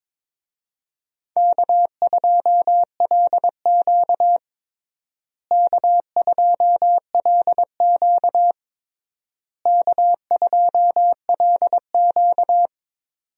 Morse code, heard as a keyed tone, about 22 wpm, 700 Hz. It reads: K2LQ K2LQ K2LQ
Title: USA Call Signs - 4 Characters - Repeated 3x - 22wpm